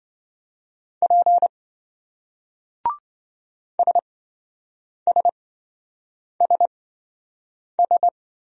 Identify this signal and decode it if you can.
Morse code, heard as a keyed tone, about 26 wpm, spaced as for 7 wpm, 700 Hz. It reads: P SSSS